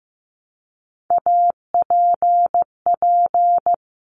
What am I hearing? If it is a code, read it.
Morse code, heard as a keyed tone, about 15 wpm, 700 Hz. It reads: APP
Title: Top Three Letter Combinations in Words 15wpm